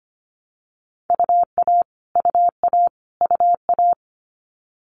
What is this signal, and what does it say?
Morse code, heard as a keyed tone, about 25 wpm, 700 Hz. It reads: UA UA UA
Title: Multi-letter Phonemes Repeated 3x 25wpm